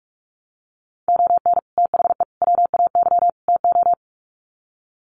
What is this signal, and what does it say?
Morse code, heard as a keyed tone, about 44 wpm, 700 Hz. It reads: ON THE WAY TO
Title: Sets of 4 Words 45wpm